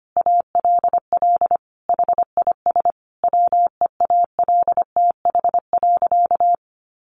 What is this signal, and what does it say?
Morse code, heard as a keyed tone, about 25 wpm, 700 Hz. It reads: ALL HIS WEALTH.
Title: Sentences from Top 1200 Words 25wpm